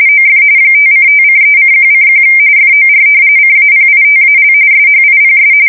RTTY, about 45.45 baud. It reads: OF 5:5:3:1:1 FOR CAPITAL SHIPS
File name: fsk170 / usnavy4